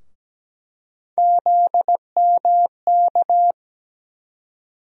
Morse code, heard as a keyed tone, about 17 wpm, 700 Hz. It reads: ZMK
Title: USA Call Sign Postfixes - 3 Letters 17wpm